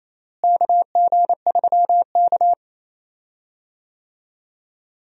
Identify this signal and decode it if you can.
Morse code, heard as a keyed tone, about 28 wpm, 700 Hz. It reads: KG3K
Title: USA Call Signs - 4 Characters - Repeated 2x - 28wpm